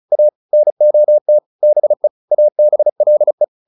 Morse code, heard as a keyed tone, about 34 wpm, 600 Hz. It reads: A NOT BE ABLE